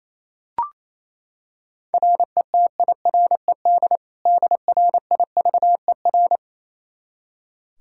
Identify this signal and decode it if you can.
Morse code, heard as a keyed tone, about 27 wpm, 700 Hz. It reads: RETIRED DRIVER
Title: K1USN SST QSO Practice 28wpm